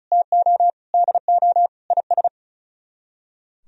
Morse code, heard as a keyed tone, about 43 wpm, 700 Hz. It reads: TO DO IS
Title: Speed-Racing - Sets of 5 Words 35wpm